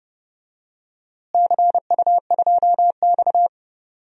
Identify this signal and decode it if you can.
Morse code, heard as a keyed tone, about 30 wpm, 700 Hz. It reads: CU2X
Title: DXCC-100 Call Signs - 4 Characters - Repeated 3x - 30wpm